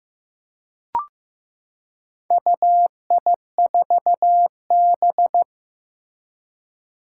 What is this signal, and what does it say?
Morse code, heard as a keyed tone, about 15 wpm, 700 Hz. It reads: UI4B
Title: Four Challenging Letter-Number Combinations 15wpm